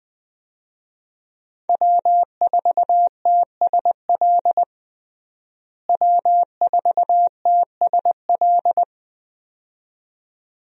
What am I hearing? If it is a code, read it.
Morse code, heard as a keyed tone, about 21 wpm, 700 Hz. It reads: W4TSL W4TSL
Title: USA Call Signs - 5 Characters - Repeated 2x - 20wpm